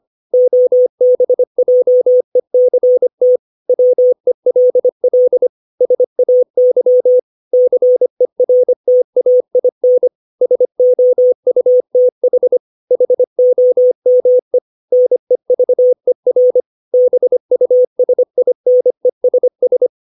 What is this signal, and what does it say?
Morse code, heard as a keyed tone, about 25 wpm, 500 Hz. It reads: OBJECT WELL SAY CERTAIN SOUTH HOME NEVER BUSINESS
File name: 632 words / 632 Words 25WPM